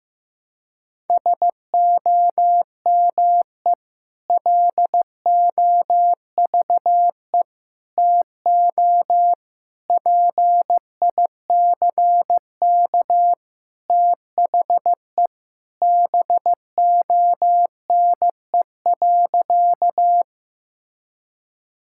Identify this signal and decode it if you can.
Morse code, heard as a keyed tone, about 15 wpm, 700 Hz. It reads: SOME LOVE TO PICK THE BONE.
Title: Sentences from Top 1600 Words 15wpm